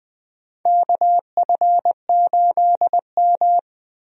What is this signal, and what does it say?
Morse code, heard as a keyed tone, about 20 wpm, 700 Hz. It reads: KF8M
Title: USA Call Signs - 4 Characters - Repeated 2x - 20wpm